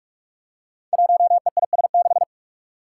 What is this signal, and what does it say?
Morse code, heard as a keyed tone, about 44 wpm, 700 Hz. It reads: 1EIS6
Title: Five Challenging Letter-Number Combinations 45wpm